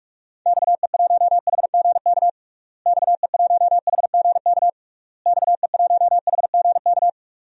Morse code, heard as a keyed tone, about 46 wpm, 700 Hz. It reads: XE1HGK XE1HGK XE1HGK
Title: DXCC-100 Call Signs - 6 Characters - Repeated 3x - 45wpm